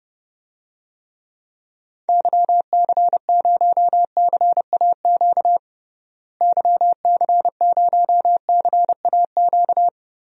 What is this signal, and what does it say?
Morse code, heard as a keyed tone, about 32 wpm, 700 Hz. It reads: YC0CAQ YC0CAQ
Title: DXCC-100 Call Signs - 6 Characters - Repeated 2x - 30wpm